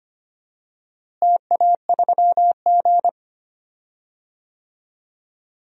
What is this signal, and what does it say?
Morse code, heard as a keyed tone, about 25 wpm, 700 Hz. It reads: TA3G
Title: DXCC-100 Call Signs - 4 Characters - Repeated 2x - 25wpm